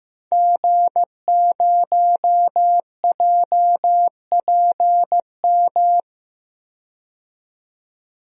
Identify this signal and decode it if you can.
Morse code, heard as a keyed tone, about 15 wpm, 700 Hz. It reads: G0JPM
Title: DXCC-100 Call Signs - 5 Characters - 15wpm